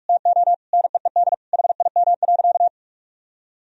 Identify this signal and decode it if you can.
Morse code, heard as a keyed tone, about 41 wpm, 700 Hz. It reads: TO NEED HIM.
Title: Sentences from Top 1000 Words - Review Part 1 - 45wpm